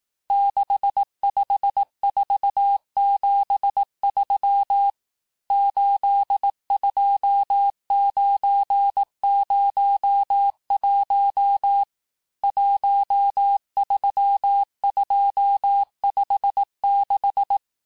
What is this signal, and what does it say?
Morse code, heard as a keyed tone, about 18 wpm, 800 Hz. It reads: 65473 82901 13256
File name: cours-du-lundi files / cours-1852-texte-chiffre-vitesse-17MPM